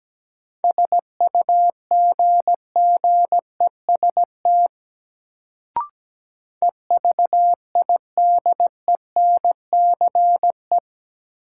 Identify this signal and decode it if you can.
Morse code, heard as a keyed tone, about 17 wpm, 700 Hz. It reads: SUGGEST EVIDENCE
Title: Sentences from Top 1300 Words - New Word Drill Alt - 17wpm